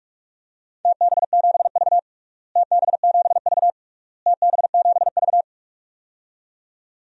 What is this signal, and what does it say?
Morse code, heard as a keyed tone, about 46 wpm, 700 Hz. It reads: TB7V TB7V TB7V
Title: DXCC-100 Call Signs - 4 Characters - Repeated 3x - 45wpm